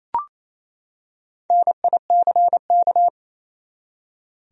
Morse code, heard as a keyed tone, about 28 wpm, 700 Hz. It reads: NICK
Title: USA Common Names - Max 4 Letters - 28wpm